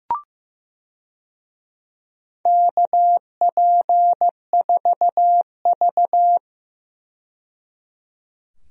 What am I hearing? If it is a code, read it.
Morse code, heard as a keyed tone, about 16 wpm, 700 Hz. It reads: KP4V
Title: DXCC-100 Call Signs - 4 Characters - 15wpm